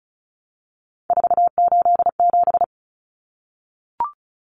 Morse code, heard as a keyed tone, about 34 wpm, 700 Hz. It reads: 487
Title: Common Cut Numbers - 3-Digit 35wpm